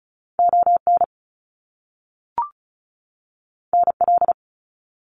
Morse code, heard as a keyed tone, about 36 wpm, 700 Hz. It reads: ON NL
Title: Canadian Provinces and Territories 35wpm